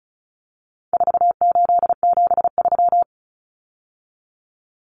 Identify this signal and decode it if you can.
Morse code, heard as a keyed tone, about 34 wpm, 700 Hz. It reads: 4873
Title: Common Cut Numbers - 4-Digit 35wpm